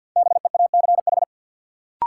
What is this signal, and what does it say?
Morse code, heard as a keyed tone, about 52 wpm, 700 Hz. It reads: BEACH
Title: Sentences from Top 1500 Words - New Word Drill Alt - 50wpm